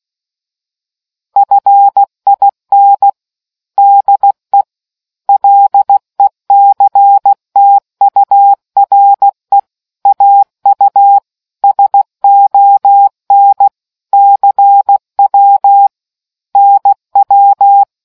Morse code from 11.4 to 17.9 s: SON CW NW